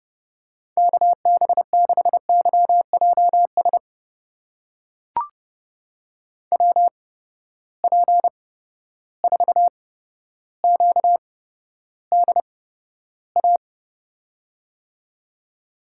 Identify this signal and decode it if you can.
Morse code, heard as a keyed tone, about 31 wpm, spaced as for 13 wpm, 700 Hz. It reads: KB6YJS WP4QDA
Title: USA Call Signs - 6 Characters - 8x Character Spacing - 30wpm